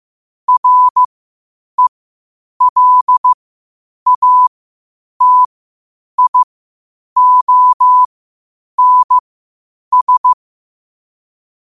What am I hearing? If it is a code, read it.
Morse code, heard as a keyed tone, about 15 wpm, spaced as for 8 wpm, 1000 Hz. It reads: RELATIONS
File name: qotdfiles / 8wpm